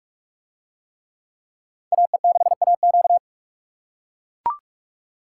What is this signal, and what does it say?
Morse code, heard as a keyed tone, about 44 wpm, 700 Hz. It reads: AE6AQ